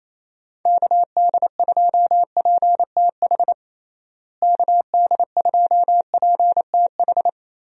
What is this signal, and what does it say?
Morse code, heard as a keyed tone, about 27 wpm, 700 Hz. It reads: KD2PTH KD2PTH
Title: USA Call Signs - 6 Characters - Repeated 3x - 28wpm